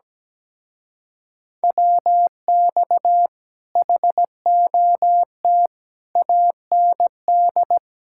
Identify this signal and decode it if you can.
Morse code, heard as a keyed tone, about 17 wpm, 700 Hz. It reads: WX HOT AND
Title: Speed-Racing Common QSO Phrases Encore - 17wpm